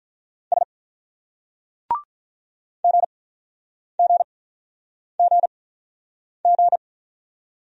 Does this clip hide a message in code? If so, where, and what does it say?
Yes: Morse code from 0.5 to 6.8 s, heard as a keyed tone, about 44 wpm, spaced as for 9 wpm, 700 Hz. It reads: S GGGG